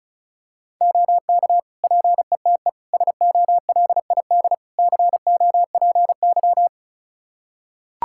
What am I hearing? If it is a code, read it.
Morse code, heard as a keyed tone, about 34 wpm, 700 Hz. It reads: OK PETE SOLID COPY
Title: Practice QSO - Exchange 3 - More Info - 35wpm